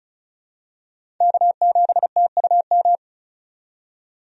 Morse code, heard as a keyed tone, about 34 wpm, 700 Hz. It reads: K7TUM
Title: USA Call Signs - 5 Characters - 35wpm